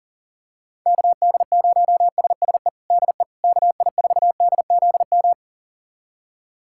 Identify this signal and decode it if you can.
Morse code, heard as a keyed tone, about 40 wpm, 700 Hz. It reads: KD0SSE DE KI4DZM
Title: USA Callsign DE Callsign 40wpm